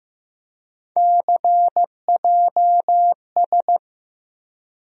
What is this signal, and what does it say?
Morse code, heard as a keyed tone, about 15 wpm, 700 Hz. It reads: CJS
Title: USA Call Sign Postfixes - 3 Letters 15wpm